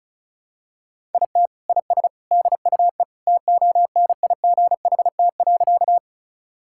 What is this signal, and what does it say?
Morse code, heard as a keyed tone, about 36 wpm, 700 Hz. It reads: IT IS DUE TONIGHT.
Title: Sentences from Top 1000 Words - Review Part 6 - 35wpm